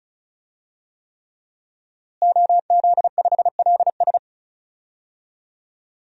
Morse code, heard as a keyed tone, about 36 wpm, 700 Hz. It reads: OZ5LS
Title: DXCC-100 Call Signs - 5 Characters - Repeated 3x - 35wpm